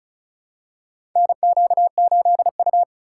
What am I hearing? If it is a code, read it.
Morse code, heard as a keyed tone, about 36 wpm, 700 Hz. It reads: NQ8U
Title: USA Call Signs - 4x Character Spacing - 35wpm